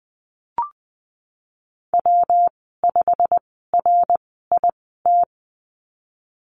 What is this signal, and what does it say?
Morse code, heard as a keyed tone, about 20 wpm, spaced as for 15 wpm, 700 Hz. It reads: W5RIT
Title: 128 Lesson 21g - Number 6 - Call Signs 20wpm